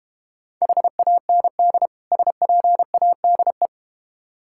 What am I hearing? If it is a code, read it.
Morse code, heard as a keyed tone, about 32 wpm, 700 Hz. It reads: HAND SPADE